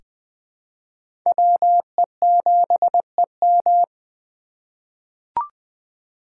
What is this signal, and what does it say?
Morse code, heard as a keyed tone, about 20 wpm, 700 Hz. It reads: WE7EM
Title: USA Call Signs - 5 Characters - 2x Character Spacing - 20wpm